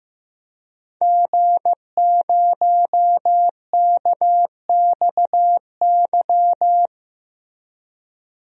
Morse code, heard as a keyed tone, about 15 wpm, 700 Hz. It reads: G0KXY
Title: DXCC-100 Call Signs - 5 Characters - Repeated 2x - 15wpm